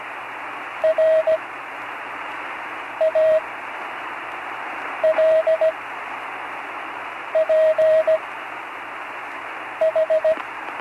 Morse code, heard as a keyed tone, about 17 wpm, spaced as for 5 wpm, 640 Hz, with radio noise. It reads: RALPH
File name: CW Learning Programs / Learning Morse Code (from Tune in the World) / 43 on the air